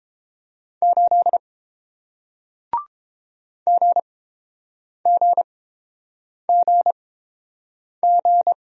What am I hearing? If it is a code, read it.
Morse code, heard as a keyed tone, about 30 wpm, spaced as for 7 wpm, 690 Hz. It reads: 8 GGGG